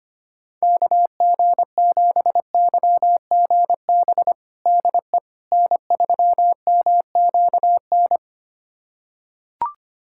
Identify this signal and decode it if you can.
Morse code, heard as a keyed tone, about 25 wpm, 700 Hz. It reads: KG7YGB DE N3MQN
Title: USA Callsign DE Callsign 25wpm